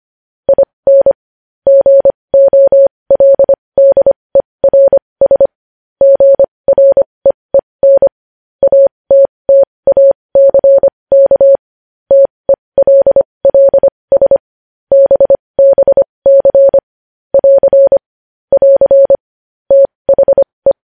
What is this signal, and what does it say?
Morse code, heard as a keyed tone, about 25 wpm, spaced as for 20 wpm, 550 Hz. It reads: IN GOLDERS GREEN ATTACK TELLS BBC <AR> <AR> THE
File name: podcast / News headlines-1x-25x20